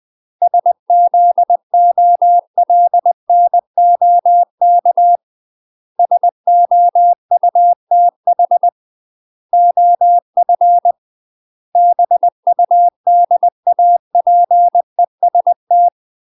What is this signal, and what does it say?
Morse code, heard as a keyed tone, about 20 wpm, 700 Hz. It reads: SZOLNOK SOUTH OF BUDAPEST
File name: UK MASTER QSOs TRAINING / 20WPM / UK MASTER QSOs 0046